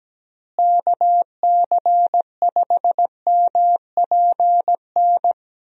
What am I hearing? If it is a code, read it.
Morse code, heard as a keyed tone, about 17 wpm, 700 Hz. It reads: KC5MPN
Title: USA Call Signs - 6 Characters - Repeated 2x - 17wpm